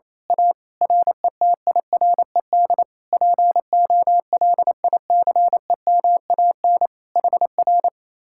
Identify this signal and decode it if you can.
Morse code, heard as a keyed tone, about 26 wpm, 700 Hz. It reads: A RETIRED POLICEMAN HR
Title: Speed-Racing Common QSO Phrases - 28wpm